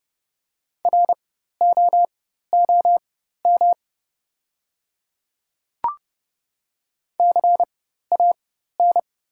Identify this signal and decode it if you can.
Morse code, heard as a keyed tone, about 30 wpm, spaced as for 14 wpm, 700 Hz. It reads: ROOM CAN
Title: 4x Character Spacing - Top 300 Words - 30wpm